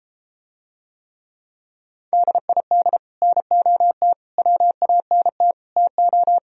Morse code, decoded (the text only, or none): DID NOT WANT TO